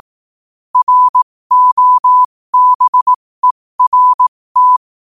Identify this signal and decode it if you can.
Morse code, heard as a keyed tone, about 18 wpm, spaced as for 15 wpm, 1000 Hz. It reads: ROBERT
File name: qotdfiles / 15wpm